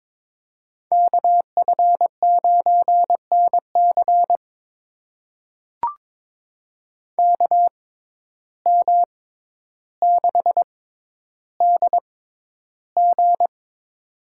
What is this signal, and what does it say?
Morse code, heard as a keyed tone, about 23 wpm, spaced as for 11 wpm, 700 Hz. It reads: KF9NC KM6DG
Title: USA Call Signs - 5 Characters - 6x Character Spacing - 22wpm